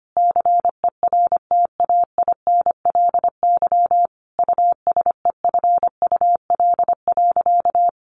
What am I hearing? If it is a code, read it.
Morse code, heard as a keyed tone, about 25 wpm, 700 Hz. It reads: CERTAINLY USEFUL.
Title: Sentences from Top 1100 Words 25wpm